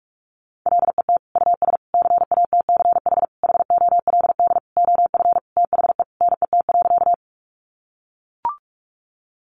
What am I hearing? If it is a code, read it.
Morse code, heard as a keyed tone, about 44 wpm, 700 Hz. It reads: LET US CATCH HOLD OF THE NET.
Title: Sentences from Top 1500 Words 45wpm